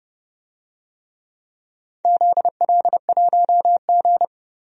Morse code, heard as a keyed tone, about 30 wpm, 700 Hz. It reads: ZL1G
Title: DXCC-100 Call Signs - 4 Characters - 30wpm